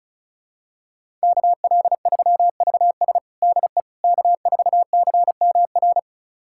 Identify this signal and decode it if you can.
Morse code, heard as a keyed tone, about 36 wpm, 700 Hz. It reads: KL3VS DE K4CMR
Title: USA Callsign DE Callsign - 5 Characters - 35wpm